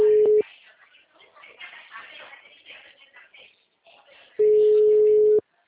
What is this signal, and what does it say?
Ringback tone (Europe, 425 Hz).